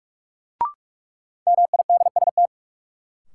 Morse code, heard as a keyed tone, about 43 wpm, 700 Hz. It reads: MIDST
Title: Sets of 4 Words - New Word Drill - 45wpm